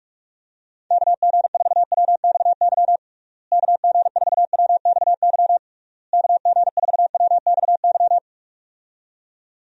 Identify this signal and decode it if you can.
Morse code, heard as a keyed tone, about 46 wpm, 700 Hz. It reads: KG4WXY KG4WXY KG4WXY
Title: USA Call Signs - 6 Characters - Repeated 3x - 45wpm